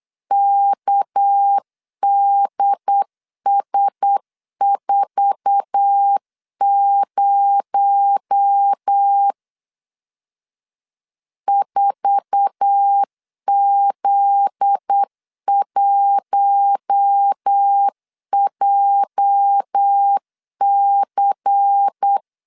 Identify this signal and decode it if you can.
Morse code, heard as a keyed tone, about 8 wpm, 800 Hz. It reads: KDS40 4Z1JC
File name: LEZIONI / 8 YJ45 / 8 YJ4 8wpm 3246